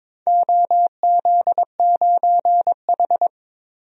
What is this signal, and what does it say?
Morse code, heard as a keyed tone, about 21 wpm, 700 Hz. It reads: OZ9H
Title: DXCC-100 Call Signs - 4 Characters - 22wpm